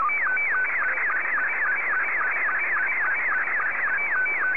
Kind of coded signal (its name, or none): SSTV (slow-scan television)